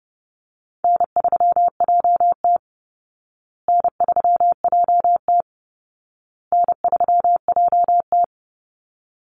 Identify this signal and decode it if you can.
Morse code, heard as a keyed tone, about 30 wpm, 700 Hz. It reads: N3JT N3JT N3JT
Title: Easy Call Signs Repeated 3x - 30wpm